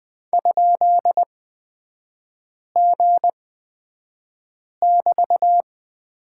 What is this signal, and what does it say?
Morse code, heard as a keyed tone, about 20 wpm, spaced as for 5 wpm, 700 Hz. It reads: ?G<BT>
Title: 243 Lesson 39e - Letter z - Three Characters 20wpm